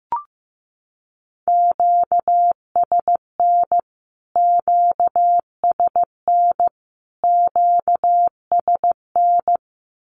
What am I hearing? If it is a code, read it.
Morse code, heard as a keyed tone, about 15 wpm, 700 Hz. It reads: QSN QSN QSN
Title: Q-Codes Repeated 3x 15wpm